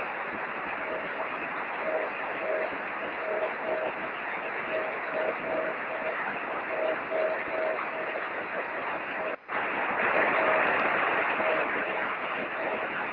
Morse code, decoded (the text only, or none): none